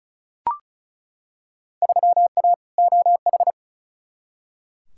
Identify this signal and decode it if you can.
Morse code, heard as a keyed tone, about 36 wpm, 700 Hz. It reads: 3A OH
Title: Field Day Exchanges - 35wpm